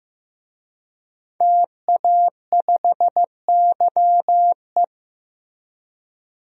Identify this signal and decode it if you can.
Morse code, heard as a keyed tone, about 15 wpm, 700 Hz. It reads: TA5YE
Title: DXCC-100 Call Signs - 5 Characters - Repeated 3x - 15wpm